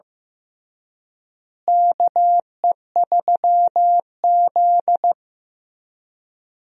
Morse code, heard as a keyed tone, about 15 wpm, 700 Hz. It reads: KE3Z
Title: USA Call Signs - 4 Characters - Repeated 3x - 15wpm